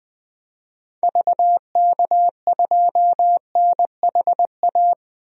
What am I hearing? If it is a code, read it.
Morse code, heard as a keyed tone, about 20 wpm, 700 Hz. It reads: VK2NHA
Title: DXCC-100 Call Signs - 6 Characters - 20wpm